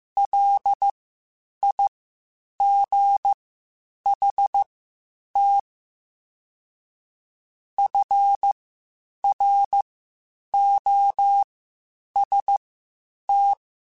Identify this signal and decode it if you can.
Morse code, heard as a keyed tone, about 15 wpm, spaced as for 8 wpm, 800 Hz. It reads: LIGHT FROST
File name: Spaced Characters / Spaced Characters-mp3s / Weather 4 24 sp @13